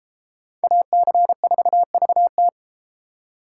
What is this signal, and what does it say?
Morse code, heard as a keyed tone, about 31 wpm, 700 Hz. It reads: AC4VT